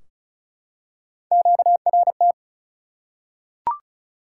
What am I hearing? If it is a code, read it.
Morse code, heard as a keyed tone, about 34 wpm, 700 Hz. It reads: QRT
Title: Top 180 QSO Elements 35wpm